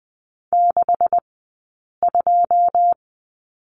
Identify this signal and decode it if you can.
Morse code, heard as a keyed tone, about 20 wpm, spaced as for 8 wpm, 700 Hz. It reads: 62
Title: 126 Lesson 21d - Number 6 - Two Characters 20wpm